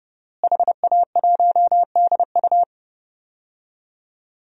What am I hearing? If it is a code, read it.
Morse code, heard as a keyed tone, about 30 wpm, 700 Hz. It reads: HA1DU
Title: DXCC-100 Call Signs - 5 Characters - Repeated 2x - 30wpm